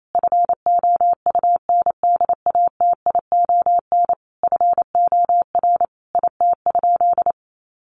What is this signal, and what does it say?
Morse code, heard as a keyed tone, about 28 wpm, 700 Hz. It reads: FOUNDATION FOR IT?
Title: Sentences from Top 1200 Words 28wpm